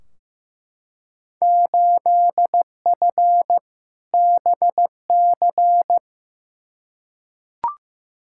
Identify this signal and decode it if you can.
Morse code, heard as a keyed tone, about 15 wpm, 700 Hz. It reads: 8F BC
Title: Field Day Exchanges Encore - 15wpm